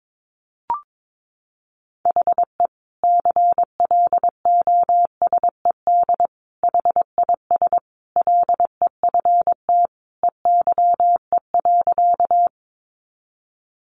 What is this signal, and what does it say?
Morse code, heard as a keyed tone, about 22 wpm, 700 Hz. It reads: HE CLOSED HIS LEFT EYE.
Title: Sentences from Top 1400 Words 22wpm